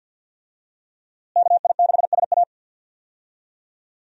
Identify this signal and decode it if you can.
Morse code, heard as a keyed tone, about 46 wpm, 700 Hz. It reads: KI6SA
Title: USA Call Signs - 5 Characters - Repeated 2x - 50wpm